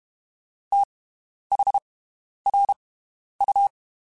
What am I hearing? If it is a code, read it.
Morse code, heard as a keyed tone, about 32 wpm, spaced as for 11 wpm, 800 Hz. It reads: THRU
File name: (spaced) / QSOs Spaced-mp3s / WD0FBT 1 @30 sp